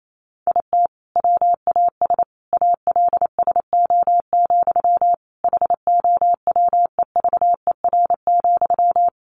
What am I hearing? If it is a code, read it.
Morse code, heard as a keyed tone, about 28 wpm, 700 Hz. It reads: IT WAS ALSO, HOWEVER,
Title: Sentences from Top 1400 Words 28wpm